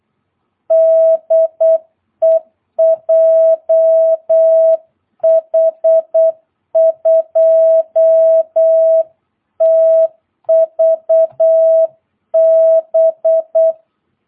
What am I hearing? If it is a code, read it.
Morse code, heard as a keyed tone, about 8 wpm, 650 Hz. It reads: DE JH2TVB
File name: jh8cbh / 2022cw